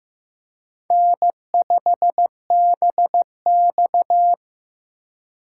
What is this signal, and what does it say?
Morse code, heard as a keyed tone, about 16 wpm, 700 Hz. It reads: N5BX
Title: USA Call Signs - 4 Characters - 4x Character Spacing - 15wpm